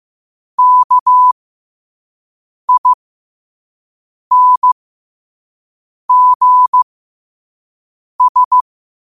Morse code, heard as a keyed tone, about 15 wpm, spaced as for 5 wpm, 1000 Hz. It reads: KINGS
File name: qotdfiles / 5wpm